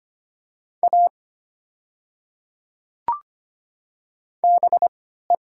Morse code, heard as a keyed tone, about 25 wpm, spaced as for 14 wpm, 700 Hz. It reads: A BE
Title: 3x Character Spacing - Top 100 Words - 25wpm